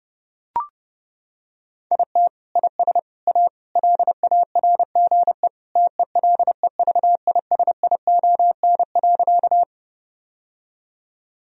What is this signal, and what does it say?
Morse code, heard as a keyed tone, about 32 wpm, 700 Hz. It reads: IT IS A LARGE TELEVISION.
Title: Sentences from Top 1000 Words - Review Part 4 - 30wpm